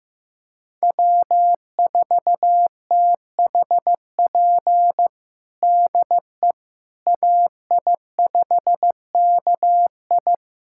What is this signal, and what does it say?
Morse code, heard as a keyed tone, about 15 wpm, 700 Hz. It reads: W4THP DE AI5KI
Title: USA Callsign DE Callsign - 5 Characters - 15wpm